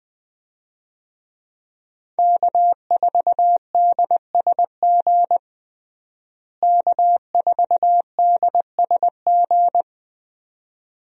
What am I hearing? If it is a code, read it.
Morse code, heard as a keyed tone, about 20 wpm, 700 Hz. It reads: K4DSG K4DSG
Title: USA Call Signs - 5 Characters - Repeated 2x - 20wpm